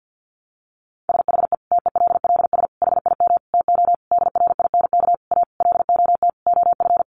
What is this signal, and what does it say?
Morse code, heard as a keyed tone, about 50 wpm, 700 Hz. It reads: SHE TELLS HIM TO DRINK A LOT OF
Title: Additional Easy Phrases 50wpm